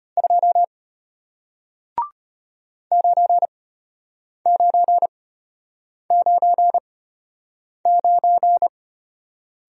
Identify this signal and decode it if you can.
Morse code, heard as a keyed tone, about 34 wpm, spaced as for 8 wpm, 700 Hz. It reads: 2 9999